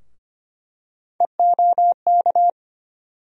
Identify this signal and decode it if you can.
Morse code, heard as a keyed tone, about 25 wpm, 700 Hz. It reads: EOK
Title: USA Call Sign Postfixes - 3 Letters 25wpm